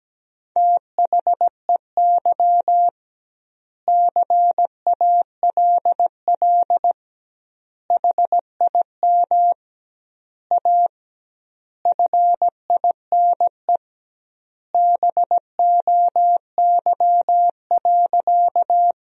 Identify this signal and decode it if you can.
Morse code, heard as a keyed tone, about 17 wpm, 700 Hz. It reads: THEY CALL HIM A FINE BOY.
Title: 2x Word-Spacing - Sentences from Top 500 Words - 17wpm